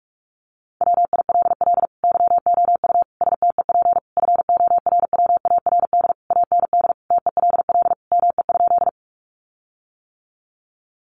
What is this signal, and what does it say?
Morse code, heard as a keyed tone, about 44 wpm, 700 Hz. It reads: WILL YOU STEP FORWARD AND TELL ME?
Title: Sentences from Top 1300 Words 45wpm